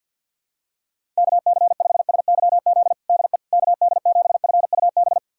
Morse code, heard as a keyed tone, about 48 wpm, 700 Hz. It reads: KC5SYZ DE KD7FUB
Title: USA Callsign DE Callsign - 6 Characters - 50wpm